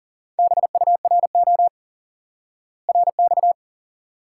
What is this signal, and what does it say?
Morse code, heard as a keyed tone, about 40 wpm, 700 Hz. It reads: BURO RX
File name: IQE Course 3 - Lesson 89 - YAGI - Two Elements - 40wpm